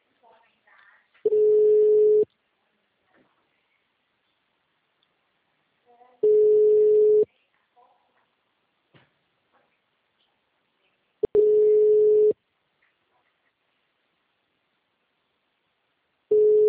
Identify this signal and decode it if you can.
Ringback tone (Europe, 425 Hz), with speech.
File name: out- -2016- - - .41